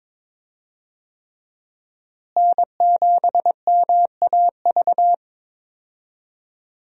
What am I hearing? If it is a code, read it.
Morse code, heard as a keyed tone, about 22 wpm, 700 Hz. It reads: N7MAV
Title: USA Call Signs - 5 Characters - Repeated 3x - 22wpm